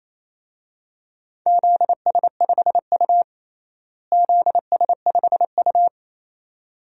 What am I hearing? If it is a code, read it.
Morse code, heard as a keyed tone, about 28 wpm, 700 Hz. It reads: ZS5U ZS5U
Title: DXCC-100 Call Signs - 4 Characters - Repeated 2x - 28wpm